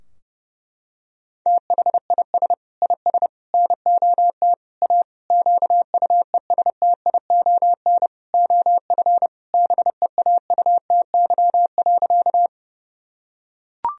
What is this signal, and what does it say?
Morse code, heard as a keyed tone, about 32 wpm, 700 Hz. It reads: THIS IS NOT A QUESTION OF BEAUTY.
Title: Sentences from Top 1000 Words - Review Part 2 - 30wpm